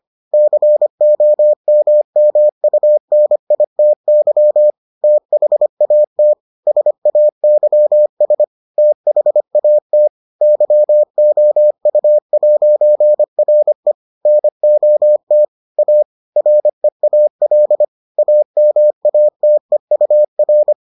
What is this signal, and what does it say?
Morse code, heard as a keyed tone, about 25 wpm, 600 Hz. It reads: COMMUNITY THAT SAYS THAT YOU'RE NOT A REAL AMATEUR